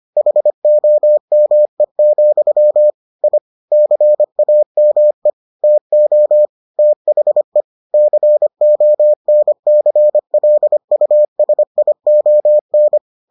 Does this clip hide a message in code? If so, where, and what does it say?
Yes: Morse code from 0.2 to 13.0 s, heard as a keyed tone, about 25 wpm, 600 Hz. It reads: HOME, I CAME TO THE CONCLUSION